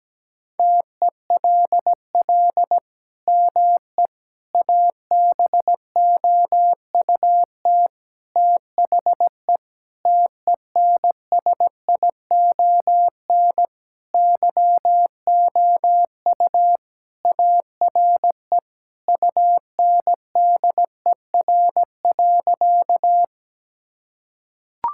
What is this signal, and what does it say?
Morse code, heard as a keyed tone, about 17 wpm, 700 Hz. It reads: TELL ME ABOUT THE TENSION YOU ARE UNDER.
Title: Sentences from Top 1600 Words 17wpm